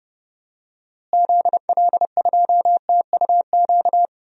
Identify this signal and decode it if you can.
Morse code, heard as a keyed tone, about 31 wpm, 700 Hz. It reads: ZL2TUQ
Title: DXCC-100 Call Signs - 6 Characters - Repeated 2x - 30wpm